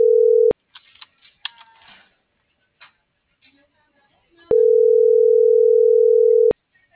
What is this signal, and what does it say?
Ringback tone (North America), 440 + 480 Hz.